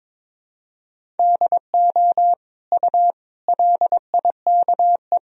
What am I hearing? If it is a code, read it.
Morse code, heard as a keyed tone, about 22 wpm, 700 Hz. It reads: DO U LIKE
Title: Sentences from Top 100 Words and Abbreviations 22wpm